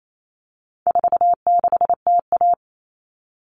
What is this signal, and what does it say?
Morse code, heard as a keyed tone, about 28 wpm, 700 Hz. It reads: 46TA
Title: All Cut Numbers - 4-Digit 28wpm